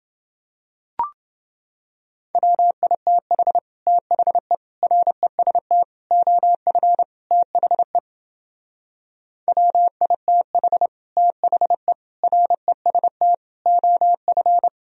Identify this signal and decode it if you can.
Morse code, heard as a keyed tone, about 30 wpm, 700 Hz. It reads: WITH THE REST OF THE WITH THE REST OF